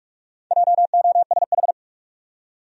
Morse code, heard as a keyed tone, about 46 wpm, 700 Hz. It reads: JOSH
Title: USA Common Names - Max 4 Letters - 45wpm